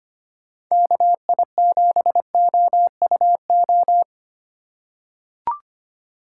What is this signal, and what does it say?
Morse code, heard as a keyed tone, about 25 wpm, 700 Hz. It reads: KI7OUO
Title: USA Call Signs - 2x Character Spacing - 25wpm